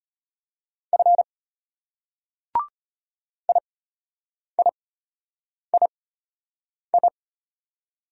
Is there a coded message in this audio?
Yes: Morse code from 0.9 to 7.1 s, heard as a keyed tone, about 33 wpm, spaced as for 8 wpm, 700 Hz. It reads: F IIII